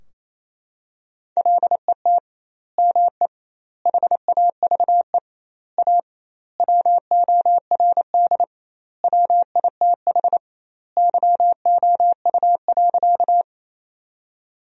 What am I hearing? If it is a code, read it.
Morse code, heard as a keyed tone, about 28 wpm, 700 Hz. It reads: LET ME HAVE A WORD WITH YOU.
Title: 2x Word-Spacing - Sentences from Top 300 Words - 28wpm